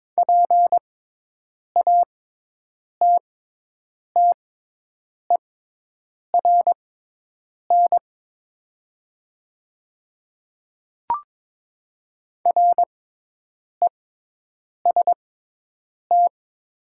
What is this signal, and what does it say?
Morse code, heard as a keyed tone, about 22 wpm, spaced as for 8 wpm, 700 Hz. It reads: PATTERN REST
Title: 6x Character Spacing - Top 500 Words - 22wpm